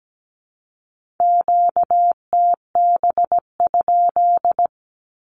Morse code, heard as a keyed tone, about 17 wpm, 700 Hz. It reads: QTB?
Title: Q-Codes Repeated 3x 17wpm